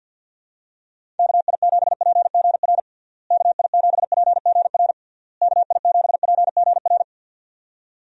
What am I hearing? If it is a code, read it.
Morse code, heard as a keyed tone, about 51 wpm, 700 Hz. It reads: KI7PGR KI7PGR KI7PGR
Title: USA Call Signs - 6 Characters - Repeated 3x - 50wpm